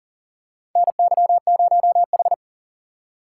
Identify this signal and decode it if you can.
Morse code, heard as a keyed tone, about 40 wpm, 700 Hz. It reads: NY0H
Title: USA Call Signs - 4 Characters - 2x Character Spacing - 40wpm